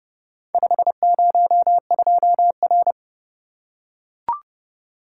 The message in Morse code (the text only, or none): 502R